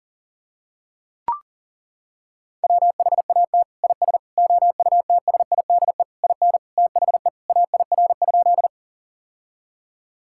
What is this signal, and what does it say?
Morse code, heard as a keyed tone, about 37 wpm, 700 Hz. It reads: WHAT IS OUTSIDE IN THE AIR?
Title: Sentences from Top 1000 Words - Review Part 6 - 40wpm